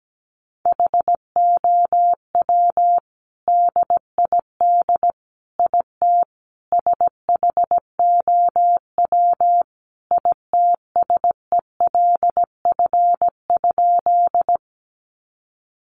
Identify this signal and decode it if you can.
Morse code, heard as a keyed tone, about 17 wpm, 700 Hz. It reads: HOW DID IT SHOW ITSELF?
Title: Sentences from Top 800 Words 17wpm